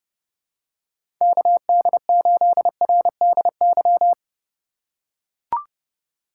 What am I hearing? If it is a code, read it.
Morse code, heard as a keyed tone, about 30 wpm, 700 Hz. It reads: KD8RDY
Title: USA Call Signs - 6 Characters - 3x Character Spacing - 30wpm